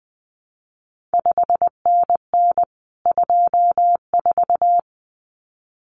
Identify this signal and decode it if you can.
Morse code, heard as a keyed tone, about 20 wpm, 700 Hz. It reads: 5NN 24
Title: CQ WW DX Contest Exchange - 20wpm